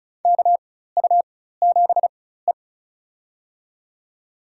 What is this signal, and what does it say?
Morse code, heard as a keyed tone, about 34 wpm, spaced as for 16 wpm, 700 Hz. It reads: KU7E
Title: USA Call Signs - 4x Character Spacing - 35wpm